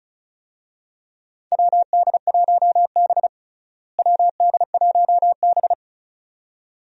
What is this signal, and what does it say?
Morse code, heard as a keyed tone, about 36 wpm, 700 Hz. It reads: WD1B WD1B
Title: USA Call Signs - 4 Characters - Repeated 2x - 35wpm